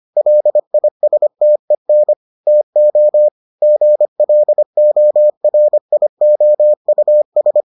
Morse code, heard as a keyed tone, about 25 wpm, 600 Hz. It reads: LISTEN TO GLORIOUS